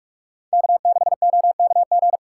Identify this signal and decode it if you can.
Morse code, heard as a keyed tone, about 47 wpm, 700 Hz. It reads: K6OKG
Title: USA Call Signs - 5 Characters - 3x Character Spacing - 45wpm